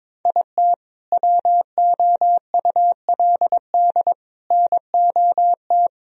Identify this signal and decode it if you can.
Morse code, heard as a keyed tone, about 21 wpm, 700 Hz. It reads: IT WOULD NOT
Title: Sentences from Top 1000 Words - Review Part 2 - 22wpm